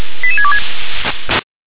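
Five-tone selective call (ZVEI), heard as quick runs of five tones, tones of about 70 ms. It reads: 88626
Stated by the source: ZVEI five-tone selective call received by SDR.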